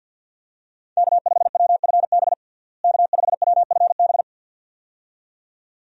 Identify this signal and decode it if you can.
Morse code, heard as a keyed tone, about 53 wpm, 700 Hz. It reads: K5WFB K5WFB
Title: USA Call Signs - 5 Characters - Repeated 2x - 50wpm